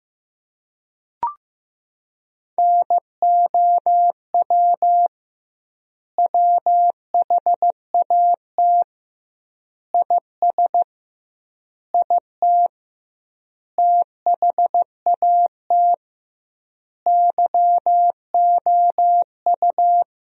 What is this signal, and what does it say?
Morse code, heard as a keyed tone, about 15 wpm, 700 Hz. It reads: NOW WHAT IS IT THAT YOU
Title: 2x Word-Spacing - Sentences from Top 200 Words - 15wpm